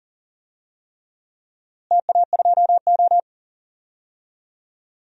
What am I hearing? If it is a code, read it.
Morse code, heard as a keyed tone, about 40 wpm, 700 Hz. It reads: TA2O
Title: DXCC-100 Call Signs - 4 Characters - Repeated 3x - 40wpm